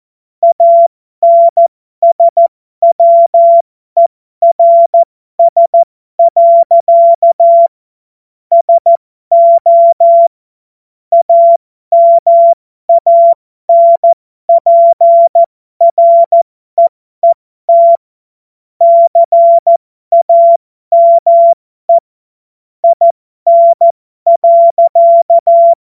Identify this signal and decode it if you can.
Morse code, heard as a keyed tone, about 14 wpm, spaced as for 12 wpm, 680 Hz. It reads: ANSWERS. SO AMANPREET CAME IN.